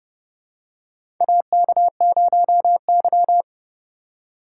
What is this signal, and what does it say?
Morse code, heard as a keyed tone, about 30 wpm, 700 Hz. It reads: AK0Y
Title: USA Call Signs - 4 Characters - 2x Character Spacing - 30wpm